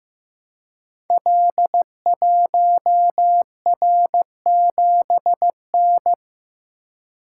Morse code, heard as a keyed tone, about 15 wpm, 700 Hz. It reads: L1R7N
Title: Code Groups - 5 Letter:Numbers - 15wpm